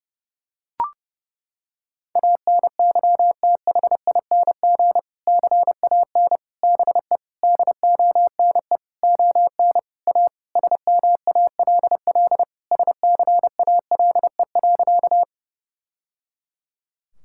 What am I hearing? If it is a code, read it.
Morse code, heard as a keyed tone, about 32 wpm, 700 Hz. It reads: ANYTHING CAN BE DONE ON A SMALL SCALE.
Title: Sentences from Top 1000 Words - Review Part 5 - 30wpm